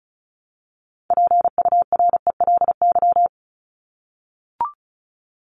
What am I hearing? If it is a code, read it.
Morse code, heard as a keyed tone, about 34 wpm, 700 Hz. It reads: PURELY
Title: LY Suffix - Top 100 Words with Base Words 35wpm